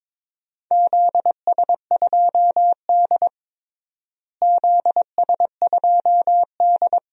Morse code, heard as a keyed tone, about 22 wpm, 700 Hz. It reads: ZS2D ZS2D
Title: DXCC-100 Call Signs - 4 Characters - Repeated 2x - 22wpm